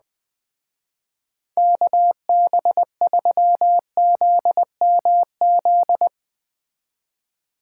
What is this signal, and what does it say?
Morse code, heard as a keyed tone, about 21 wpm, 700 Hz. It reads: KB3ZMZ